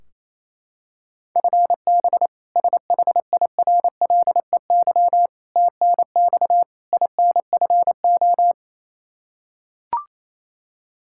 Morse code, heard as a keyed tone, about 28 wpm, 700 Hz. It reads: FB SHIRLEY TNX INFO
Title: K1USN SST QSO Practice 28wpm